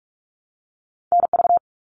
Morse code, heard as a keyed tone, about 44 wpm, 700 Hz. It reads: NV
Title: ARRL Sections 45wpm